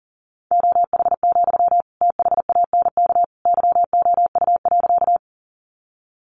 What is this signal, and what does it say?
Morse code, heard as a keyed tone, about 40 wpm, 700 Hz. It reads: OH, THANK YOU.
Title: Sentences from Top 600 Words 40wpm